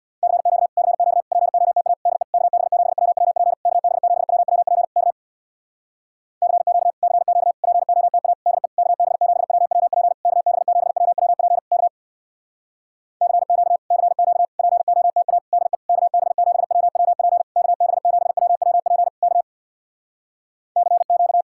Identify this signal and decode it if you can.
Morse code, heard as a keyed tone, about 66 wpm, 700 Hz. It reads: CQ CQ POTA DE KB7WWP KB7WWP K CQ CQ POTA DE KB7WWP KB7WWP K CQ CQ POTA DE KB7WWP KB7WWP K CQ